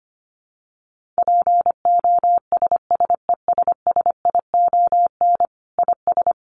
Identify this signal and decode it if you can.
Morse code, heard as a keyed tone, about 25 wpm, 700 Hz. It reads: POSSESSION IS
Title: Sentences from Top 1400 Words 25wpm